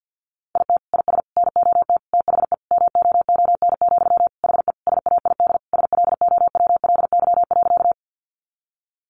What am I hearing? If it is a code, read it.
Morse code, heard as a keyed tone, about 50 wpm, 700 Hz. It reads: IT IS NOT THE MOON, HE SAID SLOWLY.
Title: Sentences from Top 1100 Words 50wpm